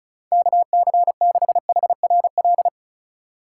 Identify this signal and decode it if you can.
Morse code, heard as a keyed tone, about 36 wpm, 700 Hz. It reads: KC6HRL
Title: USA Call Signs - 2x Character Spacing - 35wpm